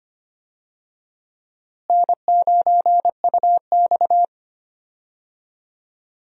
Morse code, heard as a keyed tone, about 25 wpm, 700 Hz. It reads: N9UX
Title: USA Call Signs - 4 Characters - Repeated 3x - 25wpm